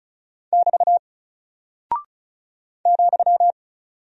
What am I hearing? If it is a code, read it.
Morse code, heard as a keyed tone, about 36 wpm, 700 Hz. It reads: <BT> ,
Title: Single Character (Letter:Number:Punctuation:Prosign) Alt 35wpm